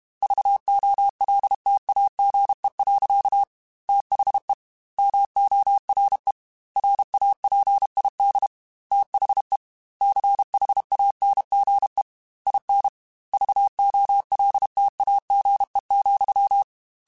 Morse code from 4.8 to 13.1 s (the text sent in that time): MORE RAPID THE CHANGE IN